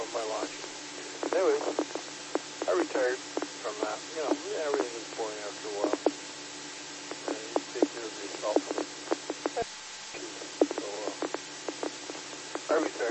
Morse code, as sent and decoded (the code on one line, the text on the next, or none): none